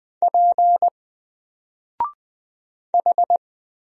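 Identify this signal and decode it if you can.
Morse code, heard as a keyed tone, about 20 wpm, 700 Hz. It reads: P H
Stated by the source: Single letter and number Morse drill at 20 wpm, from N4LCD.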